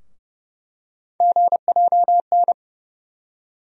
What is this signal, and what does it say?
Morse code, heard as a keyed tone, about 30 wpm, 700 Hz. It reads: GJN
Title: USA Call Sign Postfixes - 3 Letters 30wpm